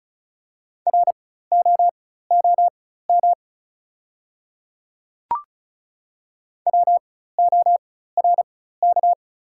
Morse code, heard as a keyed tone, about 36 wpm, spaced as for 16 wpm, 690 Hz. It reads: ROOM WORK
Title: 4x Character Spacing - Top 500 Words - 35wpm